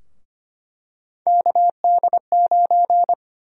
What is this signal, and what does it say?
Morse code, heard as a keyed tone, about 25 wpm, 700 Hz. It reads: KD9
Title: USA Call Sign Prefixes - 25wpm